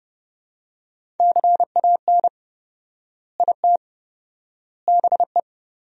Morse code, heard as a keyed tone, about 32 wpm, 700 Hz. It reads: CAN IT BE
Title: 4x Word-Spacing - Sentences from Top 500 Words - 30wpm